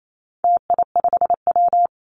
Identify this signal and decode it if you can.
Morse code, heard as a keyed tone, about 28 wpm, 700 Hz. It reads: TI5W
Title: Call Signs - 28wpm